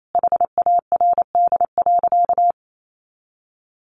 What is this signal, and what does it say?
Morse code, heard as a keyed tone, about 28 wpm, 700 Hz. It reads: HARD.
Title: Sentences from Top 300 Words 28wpm